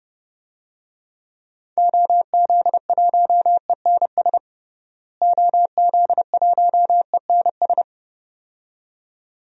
Morse code, heard as a keyed tone, about 30 wpm, 700 Hz. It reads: OZ1ENS OZ1ENS
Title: DXCC-100 Call Signs - 6 Characters - Repeated 2x - 30wpm